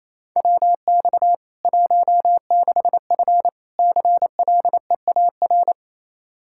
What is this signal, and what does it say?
Morse code, heard as a keyed tone, about 28 wpm, 700 Hz. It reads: WX 16F CLEAR
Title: K1USN SST QSO Practice 28wpm